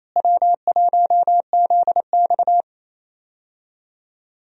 Morse code, heard as a keyed tone, about 28 wpm, 700 Hz. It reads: W1ZX
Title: USA Call Signs - 4 Characters - Repeated 2x - 28wpm